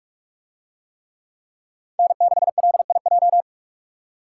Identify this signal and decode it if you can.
Morse code, heard as a keyed tone, about 44 wpm, 700 Hz. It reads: N6LIJ